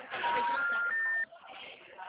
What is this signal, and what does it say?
Special information tone, with speech.
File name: out- -2011- - - .29913